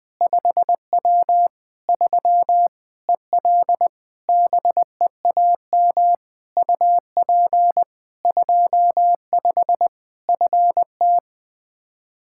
Morse code, heard as a keyed tone, about 20 wpm, 700 Hz. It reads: 5W 3 EL BEAM UP 25 FT
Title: Practice QSO - Exchange 2 - Info - 20wpm